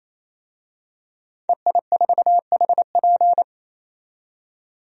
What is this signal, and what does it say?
Morse code, heard as a keyed tone, about 28 wpm, 700 Hz. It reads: EI4HP
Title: DXCC-100 Call Signs - 5 Characters - 28wpm